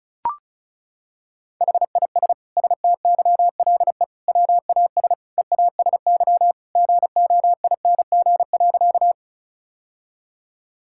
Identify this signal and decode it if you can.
Morse code, heard as a keyed tone, about 36 wpm, 700 Hz. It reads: HIS STYLE WAS EASY GOING.
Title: Sentences from Top 1000 Words - Review Part 5 - 35wpm